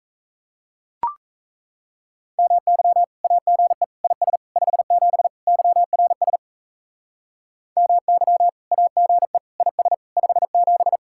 Morse code, heard as a keyed tone, about 34 wpm, 700 Hz. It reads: MY AGE IS 57 YRS MY AGE IS 57
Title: Speed-Racing Common QSO Phrases - 28wpm